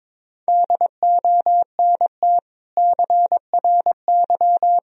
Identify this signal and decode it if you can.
Morse code, heard as a keyed tone, about 22 wpm, 700 Hz. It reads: DONT CRY
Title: Sentences from Top 1000 Words - Review Part 2 - 22wpm